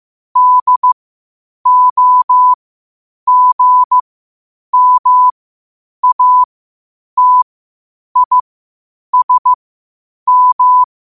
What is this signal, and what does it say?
Morse code, heard as a keyed tone, about 15 wpm, spaced as for 8 wpm, 1000 Hz. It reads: DOGMATISM
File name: qotdfiles / 8wpm